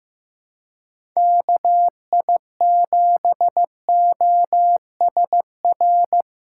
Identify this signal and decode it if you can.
Morse code, heard as a keyed tone, about 15 wpm, 700 Hz. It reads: KI7OSR
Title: USA Call Signs - 6 Characters - 4x Character Spacing - 15wpm